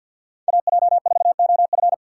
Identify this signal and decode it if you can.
Morse code, heard as a keyed tone, about 48 wpm, 700 Hz. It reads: AJ4OF
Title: USA Call Signs - 5 Characters - 4x Character Spacing - 50wpm